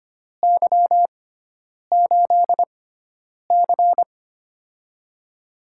Morse code, heard as a keyed tone, about 25 wpm, spaced as for 9 wpm, 700 Hz. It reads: Y8C
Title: USA Call Signs - 6x Character Spacing - 25wpm